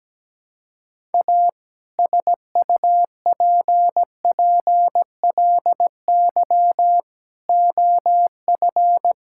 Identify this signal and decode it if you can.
Morse code, heard as a keyed tone, about 17 wpm, 700 Hz. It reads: A SUPPLY OF
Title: Sentences from Top 1000 Words - Review Part 5 - 17wpm